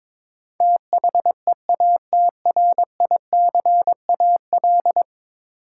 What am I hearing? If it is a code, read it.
Morse code, heard as a keyed tone, about 22 wpm, 700 Hz. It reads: THEATRICAL
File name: practice / 22wpm / 10 Letter Words - First 3 Letters Given - 22wpm